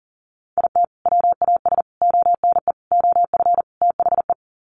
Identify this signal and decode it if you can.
Morse code, heard as a keyed tone, about 40 wpm, 700 Hz. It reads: IT WAS ONE OF THE
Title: Sets of 5 Words 40wpm